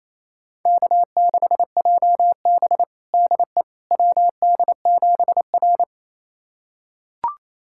Morse code, heard as a keyed tone, about 28 wpm, 690 Hz. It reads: K6JB DE WD7R
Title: USA Callsign DE Callsign - 4 Characters - 28wpm